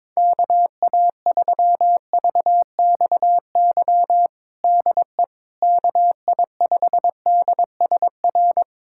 Morse code, heard as a keyed tone, about 22 wpm, 700 Hz. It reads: KA3VXY DE KI5DSR
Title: USA Callsign DE Callsign 22wpm